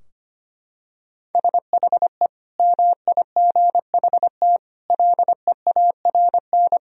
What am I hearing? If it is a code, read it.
Morse code, heard as a keyed tone, about 25 wpm, 700 Hz. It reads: SHE MIGHT LEARN
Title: Sentences from Top 1000 Words - Review Part 4 - 25wpm